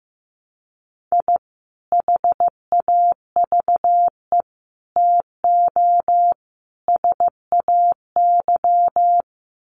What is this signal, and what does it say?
Morse code, heard as a keyed tone, about 14 wpm, 700 Hz. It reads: I HAVE TO SAY
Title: Sets of 4 Words 15wpm